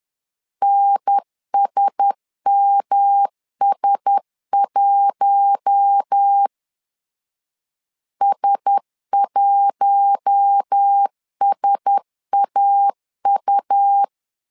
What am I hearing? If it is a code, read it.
Morse code, heard as a keyed tone, about 10 wpm, 800 Hz. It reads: NSMS1 S1SAU
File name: LEZIONI / 4 Q1S 10wpm 2133